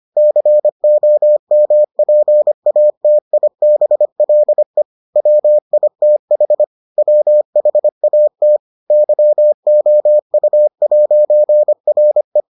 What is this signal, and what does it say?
Morse code, heard as a keyed tone, about 25 wpm, 600 Hz. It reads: COMPATIBLE WITH WHAT YOU'RE